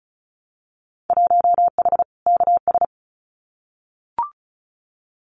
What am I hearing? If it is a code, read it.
Morse code, heard as a keyed tone, about 36 wpm, 700 Hz. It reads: 1H KS
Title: Winter Field Day Exchanges 35wpm